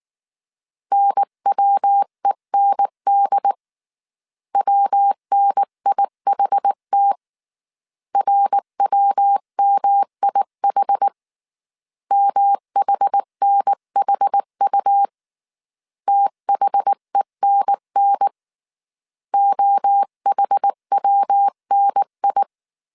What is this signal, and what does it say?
Morse code, heard as a keyed tone, about 19 wpm, 800 Hz. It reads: NWEND WNIHT RWMIH MHNHU THENN OHWNI
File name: CW / DET / 1 3 1500C S B 18wpm 1916